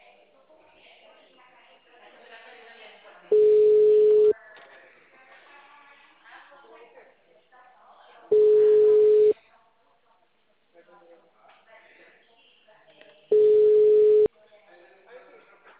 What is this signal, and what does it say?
Ringback tone (Europe, 425 Hz), with speech.